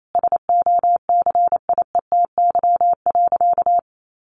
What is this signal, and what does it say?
Morse code, heard as a keyed tone, about 28 wpm, 700 Hz. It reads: SOCIETY.
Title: Sentences from Top 1500 Words 28wpm